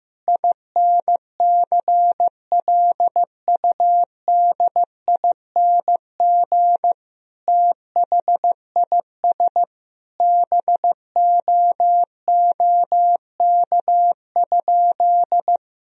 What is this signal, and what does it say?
Morse code, heard as a keyed tone, about 15 wpm, 700 Hz. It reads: INCLUDING THIS BOOK?
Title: Sentences from Top 1000 Words - Review Part 4 - 15wpm